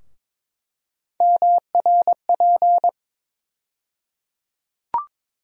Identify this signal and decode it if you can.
Morse code, heard as a keyed tone, about 22 wpm, 700 Hz. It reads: MRP
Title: USA Call Sign Postfixes - 3 Letters 22wpm